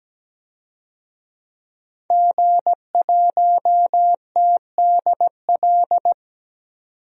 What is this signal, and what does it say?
Morse code, heard as a keyed tone, about 17 wpm, 700 Hz. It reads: G1TDL
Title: DXCC-100 Call Signs - 5 Characters - 17wpm